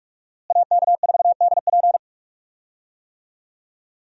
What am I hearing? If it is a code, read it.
Morse code, heard as a keyed tone, about 42 wpm, 700 Hz. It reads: AK4DP
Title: USA Call Signs - 5 Characters - Repeated 3x - 45wpm